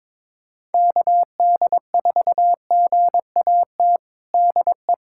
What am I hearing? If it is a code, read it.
Morse code, heard as a keyed tone, about 22 wpm, 700 Hz. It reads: KD4GAT DE
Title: USA Callsign DE Callsign - 6 Characters - 22wpm